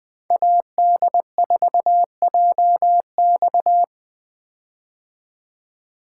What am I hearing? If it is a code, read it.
Morse code, heard as a keyed tone, about 20 wpm, 700 Hz. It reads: AD4JX
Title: 2023 POTA Hunter Call Signs - 20wpm